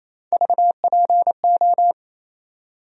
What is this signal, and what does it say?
Morse code, heard as a keyed tone, about 28 wpm, 700 Hz. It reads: VPO
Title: USA Call Sign Postfixes - 3 Letters 28wpm